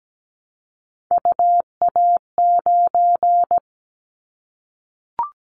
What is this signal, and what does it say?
Morse code, heard as a keyed tone, about 17 wpm, 700 Hz. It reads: UA9
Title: Callsign Prefixes for Top 25 DXCC - 17wpm